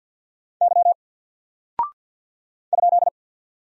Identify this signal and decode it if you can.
Morse code, heard as a keyed tone, about 48 wpm, 700 Hz. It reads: Y ?
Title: Single Character (Letter:Number:Punctuation:Prosign) Alt 50wpm